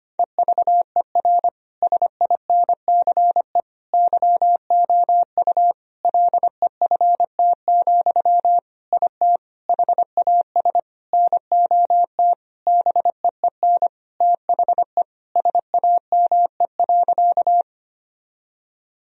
Morse code, heard as a keyed tone, about 25 wpm, 700 Hz. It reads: EVER SINCE YOU LEFT, IT HAS NOT BEEN THE SAME.
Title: Sentences from Top 1000 Words - Review Part 1 - 25wpm